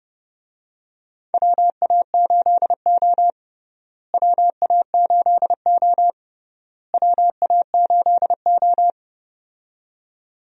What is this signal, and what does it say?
Morse code, heard as a keyed tone, about 32 wpm, 700 Hz. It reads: WA8O WA8O WA8O
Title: USA Call Signs - 4 Characters - Repeated 3x - 30wpm